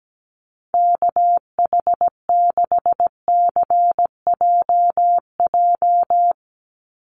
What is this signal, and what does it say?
Morse code, heard as a keyed tone, about 17 wpm, 700 Hz. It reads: KH6CJJ
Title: Call Signs Repeated 2x - 17wpm